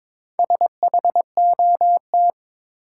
Morse code, heard as a keyed tone, about 22 wpm, 700 Hz. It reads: SHOT
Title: Sentences from Top 1100 Words - New Word Drill - 22wpm